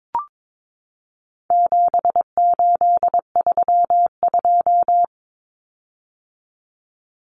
Morse code, heard as a keyed tone, about 22 wpm, 700 Hz. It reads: 7832
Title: Common Cut Numbers - 4-Digit 22wpm